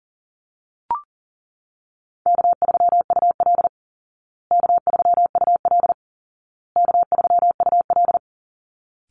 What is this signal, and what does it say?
Morse code, heard as a keyed tone, about 40 wpm, 700 Hz. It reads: K3UL K3UL K3UL
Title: Call Signs Repeated 2x - 40wpm